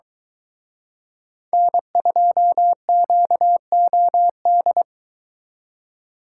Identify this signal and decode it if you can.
Morse code, heard as a keyed tone, about 23 wpm, 700 Hz. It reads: N2QOD